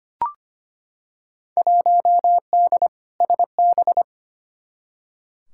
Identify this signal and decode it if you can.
Morse code, heard as a keyed tone, about 25 wpm, 700 Hz. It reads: 1D SB
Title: Field Day Exchanges - 25wpm